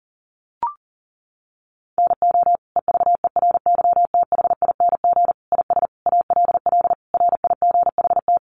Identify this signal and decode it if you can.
Morse code, heard as a keyed tone, about 40 wpm, 700 Hz. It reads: NO EVERYTHING IS ALL RIGHT
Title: Sentences from Top 700 Words 40wpm